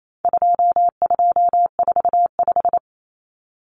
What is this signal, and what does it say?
Morse code, heard as a keyed tone, about 28 wpm, 700 Hz. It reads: 2245
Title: Numbers - 4-Digit 28wpm